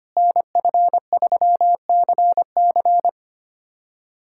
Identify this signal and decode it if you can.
Morse code, heard as a keyed tone, about 25 wpm, 700 Hz. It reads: NF3CC
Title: USA Callsign DE Callsign 25wpm